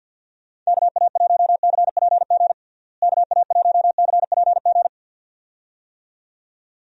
Morse code, heard as a keyed tone, about 50 wpm, 700 Hz. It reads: KA1CPG KA1CPG
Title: USA Call Signs - 6 Characters - Repeated 2x - 50wpm